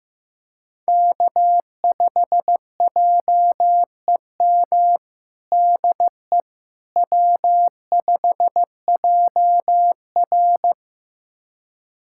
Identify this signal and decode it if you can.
Morse code, heard as a keyed tone, about 16 wpm, 700 Hz. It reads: K5JEM DE W5JR
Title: USA Callsign DE Callsign 15wpm